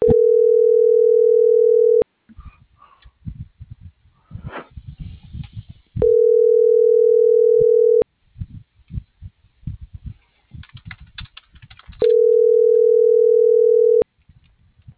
Ringback tone (North America), 440 + 480 Hz.